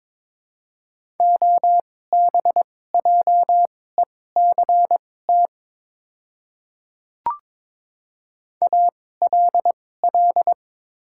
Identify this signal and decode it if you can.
Morse code, heard as a keyed tone, about 22 wpm, spaced as for 16 wpm, 700 Hz. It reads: OBJECT ALL
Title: 2x Character Spacing - Top 500 Words - 22wpm